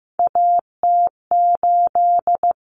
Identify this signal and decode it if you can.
Morse code, heard as a keyed tone, about 15 wpm, 700 Hz. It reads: AT8
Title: Common Cut Numbers - 3-Digit 15wpm